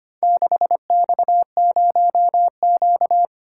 Morse code, heard as a keyed tone, about 25 wpm, 700 Hz. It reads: 6X0Q
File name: practice / 25wpm / Four Letter-Number Combinations - 25wpm